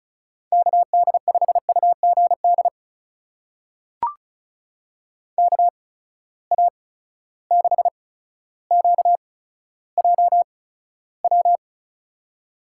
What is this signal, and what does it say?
Morse code, heard as a keyed tone, about 35 wpm, spaced as for 15 wpm, 700 Hz. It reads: KD5UGD KA6QJW
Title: USA Call Signs - 6 Characters - 8x Character Spacing - 35wpm